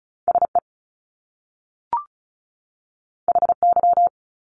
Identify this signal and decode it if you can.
Morse code, heard as a keyed tone, about 36 wpm, 700 Hz. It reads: SE HY
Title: Top Two Letter Combinations in Words 35wpm